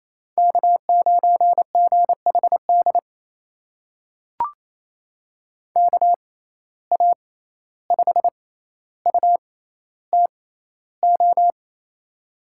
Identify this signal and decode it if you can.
Morse code, heard as a keyed tone, about 28 wpm, spaced as for 14 wpm, 700 Hz. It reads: K9GHD KA5UTO
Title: USA Call Signs - 6x Character Spacing - 28wpm